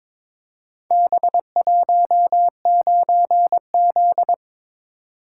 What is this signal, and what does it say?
Morse code, heard as a keyed tone, about 22 wpm, 700 Hz. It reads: B19Z
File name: practice / 22wpm / Four Letter-Number Combinations - 22wpm